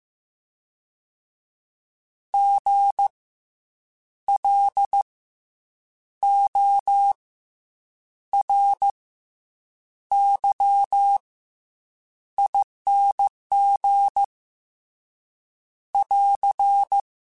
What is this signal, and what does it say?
Morse code, heard as a keyed tone, about 15 wpm, 800 Hz. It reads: G L O R Y ING <AR>
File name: Special INGs / Special INGs-mp3s / SlowToFaster ing @13